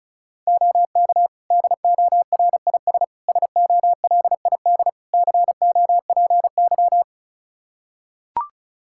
Morse code, heard as a keyed tone, about 36 wpm, 700 Hz. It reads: OK DORIS SOLID COPY
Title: Practice QSO - Exchange 3 - More Info - 35wpm